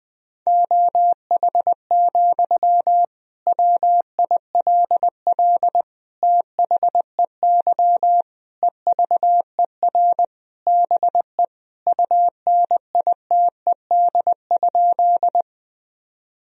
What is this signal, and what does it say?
Morse code, heard as a keyed tone, about 20 wpm, 700 Hz. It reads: OH, WILL THEY EVER BE UNITED?
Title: Sentences from Top 1000 Words - Review Part 4 - 20wpm